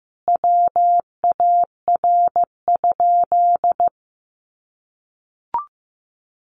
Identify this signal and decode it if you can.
Morse code, heard as a keyed tone, about 14 wpm, 700 Hz. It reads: WAR?
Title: Sentences from Top 1400 Words 15wpm